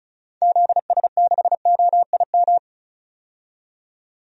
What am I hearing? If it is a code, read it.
Morse code, heard as a keyed tone, about 36 wpm, 700 Hz. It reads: ZS6OIM
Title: DXCC-100 Call Signs - 6 Characters - 35wpm